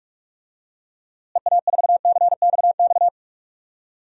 Morse code, heard as a keyed tone, about 46 wpm, 700 Hz. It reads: EA4CXX
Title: DXCC-100 Call Signs - 6 Characters - Repeated 3x - 45wpm